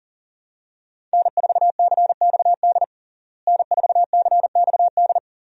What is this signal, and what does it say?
Morse code, heard as a keyed tone, about 40 wpm, 700 Hz. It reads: N4CXD N4CXD
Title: USA Call Signs - 5 Characters - Repeated 2x - 40wpm